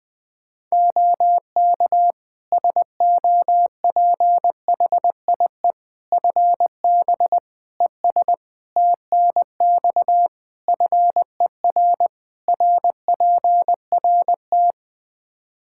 Morse code, heard as a keyed tone, about 20 wpm, 700 Hz. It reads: OK SOPHIE FB ES TNX FER RPRT
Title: Practice QSO - Exchange 2 - Info - 20wpm